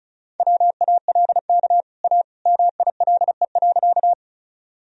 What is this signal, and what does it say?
Morse code, heard as a keyed tone, about 34 wpm, 700 Hz. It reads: WALK A MILE.
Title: Sentences from Top 1000 Words - Review Part 1 - 35wpm